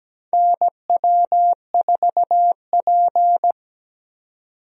Morse code, heard as a keyed tone, about 17 wpm, 700 Hz. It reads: NW4P
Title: USA Call Signs - 4 Characters - 17wpm